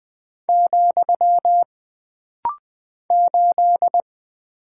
Morse code, heard as a keyed tone, about 20 wpm, 700 Hz. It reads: , 8
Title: Single Character (Letter:Number:Punctuation:Prosign) - Rapid-Fire - 20wpm